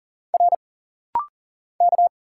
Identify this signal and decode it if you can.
Morse code, heard as a keyed tone, about 41 wpm, 700 Hz. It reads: R K
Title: Single Letters - Rapid-Fire 40wpm